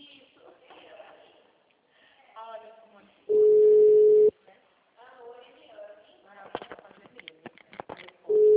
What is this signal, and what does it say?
Ringback tone (Europe, 425 Hz), with speech.
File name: out- -2003- - - .70103